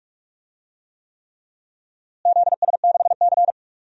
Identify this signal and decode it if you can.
Morse code, heard as a keyed tone, about 46 wpm, 700 Hz. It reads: ZS6C
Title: DXCC-100 Call Signs - 4 Characters - Repeated 3x - 45wpm